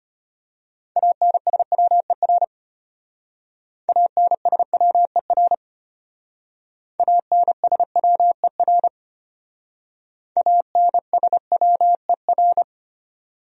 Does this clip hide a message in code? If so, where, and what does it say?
Yes: Morse code from 1.0 to 12.6 s, heard as a keyed tone, about 32 wpm, 700 Hz. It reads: ANSWER ANSWER ANSWER ANSWER